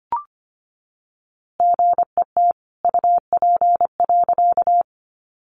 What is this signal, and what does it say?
Morse code, heard as a keyed tone, about 25 wpm, 700 Hz. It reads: GET UP.
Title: A Sound Of Thunder 25wpm